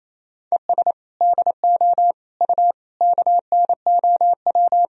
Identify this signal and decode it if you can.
Morse code, heard as a keyed tone, about 28 wpm, 700 Hz. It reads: ES DO U KNOW
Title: Sentences from Top 200 Words and Abbreviations 28wpm